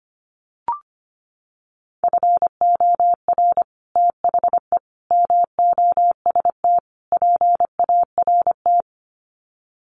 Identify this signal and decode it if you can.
Morse code, heard as a keyed tone, about 25 wpm, 700 Hz. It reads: FOR THE MOST PART
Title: Sets of 4 Words 25wpm